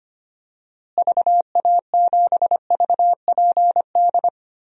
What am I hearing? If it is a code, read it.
Morse code, heard as a keyed tone, about 25 wpm, 700 Hz. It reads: VA7VPD
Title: DXCC-100 Call Signs - 6 Characters - Repeated 3x - 25wpm